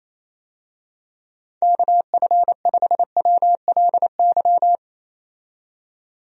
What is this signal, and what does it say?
Morse code, heard as a keyed tone, about 28 wpm, 700 Hz. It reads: KF5WLY
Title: USA Call Signs - 6 Characters - 28wpm